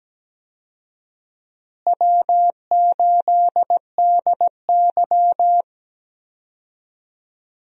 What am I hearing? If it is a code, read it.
Morse code, heard as a keyed tone, about 17 wpm, 700 Hz. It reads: W8DY
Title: USA Call Signs - 4 Characters - Repeated 2x - 17wpm